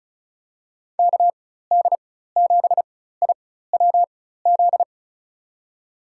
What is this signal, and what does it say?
Morse code, heard as a keyed tone, about 34 wpm, spaced as for 16 wpm, 700 Hz. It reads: KD7IWZ
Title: USA Call Signs - 6 Characters - 4x Character Spacing - 35wpm